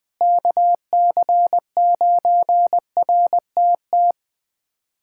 Morse code, heard as a keyed tone, about 20 wpm, 700 Hz. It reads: KC9RTT
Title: USA Callsign DE Callsign - 6 Characters - 20wpm